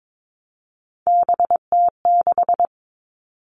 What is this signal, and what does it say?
Morse code, heard as a keyed tone, about 22 wpm, 700 Hz. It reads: BT6
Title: All Cut Numbers - 3-Digit 22wpm